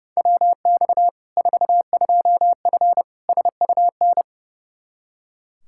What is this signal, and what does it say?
Morse code, heard as a keyed tone, about 32 wpm, 700 Hz. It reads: WX 42F SUN
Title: K1USN SST QSO Practice 30wpm